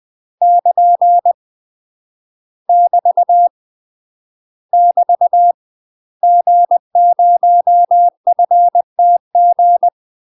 Morse code, heard as a keyed tone, about 20 wpm, 700 Hz. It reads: <KN> <BT> <BT> G0FTG
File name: UK MASTER QSOs TRAINING / 20WPM / UK MASTER QSOs 0058